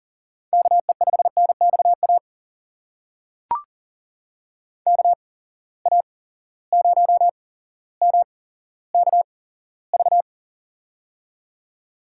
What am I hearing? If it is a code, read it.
Morse code, heard as a keyed tone, about 41 wpm, spaced as for 17 wpm, 700 Hz. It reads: KE5NXA KA0MKV
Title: USA Call Signs - 6 Characters - 8x Character Spacing - 40wpm